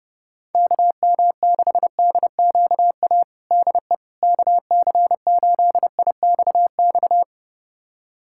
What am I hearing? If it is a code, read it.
Morse code, heard as a keyed tone, about 32 wpm, 700 Hz. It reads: KM6DQA DE KC8IXX
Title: USA Callsign DE Callsign 30wpm